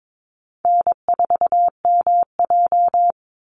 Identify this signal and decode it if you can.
Morse code, heard as a keyed tone, about 22 wpm, 700 Hz. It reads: N4MJ
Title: NA SOTA Chaser Callsigns 22wpm